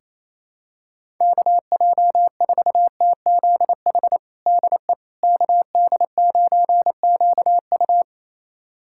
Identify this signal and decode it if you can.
Morse code, heard as a keyed tone, about 28 wpm, 700 Hz. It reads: KJ4TZH DE KD9QU
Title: USA Callsign DE Callsign 28wpm